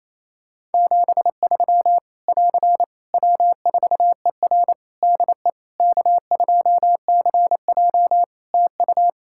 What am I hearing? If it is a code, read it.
Morse code, heard as a keyed tone, about 28 wpm, 700 Hz. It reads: 73 <AR> W4ER DE K2CJ TU
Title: Practice QSO - Exchange 4 - Ending - 28wpm